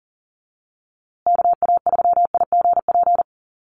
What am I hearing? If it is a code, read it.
Morse code, heard as a keyed tone, about 40 wpm, 700 Hz. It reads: KA3IGP
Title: USA Call Signs Repeated 2x 40wpm